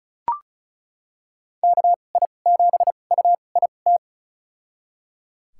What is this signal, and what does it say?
Morse code, heard as a keyed tone, about 36 wpm, spaced as for 25 wpm, 700 Hz. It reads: KI7UIT
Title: USA Call Signs - 6 Characters - 2x Character Spacing - 35wpm